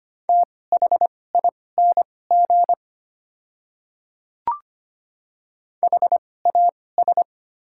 Morse code, heard as a keyed tone, about 25 wpm, spaced as for 18 wpm, 690 Hz. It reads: THING HAS
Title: 2x Character Spacing - Top 100 Words - 25wpm